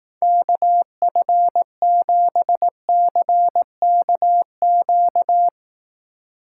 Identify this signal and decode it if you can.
Morse code, heard as a keyed tone, about 18 wpm, 700 Hz. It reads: KF7CKQ
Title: Speed Racing - USA 2x3 Call Signs 15wpm